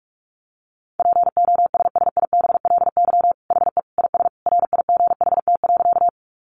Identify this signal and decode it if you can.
Morse code, heard as a keyed tone, about 44 wpm, 700 Hz. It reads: POSSIBLY HE IS RIGHT.
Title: Sentences from Top 1300 Words 45wpm